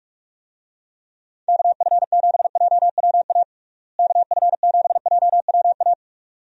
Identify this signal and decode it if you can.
Morse code, heard as a keyed tone, about 46 wpm, 700 Hz. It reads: KF7JWA KF7JWA
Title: USA Call Signs - 6 Characters - Repeated 2x - 45wpm